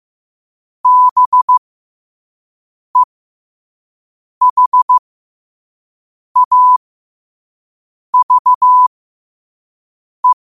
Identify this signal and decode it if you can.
Morse code, heard as a keyed tone, about 15 wpm, spaced as for 5 wpm, 1000 Hz. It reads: BEHAVE
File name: 5wpm